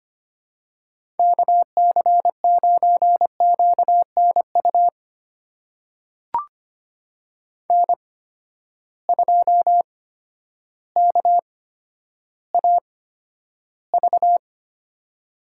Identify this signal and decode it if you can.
Morse code, heard as a keyed tone, about 25 wpm, spaced as for 11 wpm, 700 Hz. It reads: KC9QNU N2KAV
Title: USA Call Signs - 8x Character Spacing - 25wpm